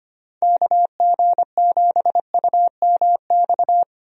Morse code, heard as a keyed tone, about 25 wpm, 700 Hz. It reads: KG7UMX
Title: USA Call Signs - 6 Characters - Repeated 3x - 25wpm